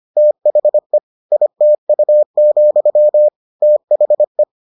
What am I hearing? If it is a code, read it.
Morse code, heard as a keyed tone, about 25 wpm, 600 Hz. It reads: THE ITU, THE